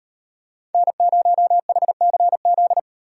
Morse code, heard as a keyed tone, about 38 wpm, 700 Hz. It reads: N0HCZ